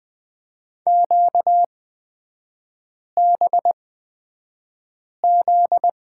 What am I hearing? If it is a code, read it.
Morse code, heard as a keyed tone, about 20 wpm, spaced as for 5 wpm, 700 Hz. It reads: QBZ
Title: 243 Lesson 39e - Letter z - Three Characters 20wpm